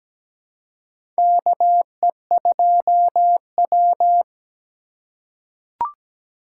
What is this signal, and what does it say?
Morse code, heard as a keyed tone, about 17 wpm, 700 Hz. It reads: KE2W
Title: USA Call Signs - 4 Characters - 2x Character Spacing - 17wpm